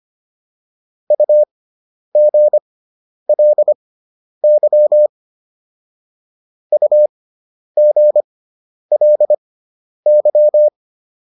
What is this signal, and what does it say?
Morse code, heard as a keyed tone, about 25 wpm, spaced as for 10 wpm, 600 Hz. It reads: UGLY UGLY